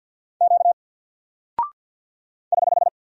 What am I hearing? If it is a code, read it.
Morse code, heard as a keyed tone, about 48 wpm, 700 Hz. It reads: Q <HH>
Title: Single Character (Letter:Number:Punctuation:Prosign) 50wpm